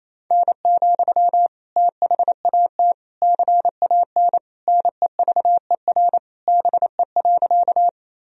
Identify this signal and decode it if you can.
Morse code, heard as a keyed tone, about 28 wpm, 700 Hz. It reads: N, THAT CAN NEVER BE.
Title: Sentences from Top 300 Words and Abbreviations 28wpm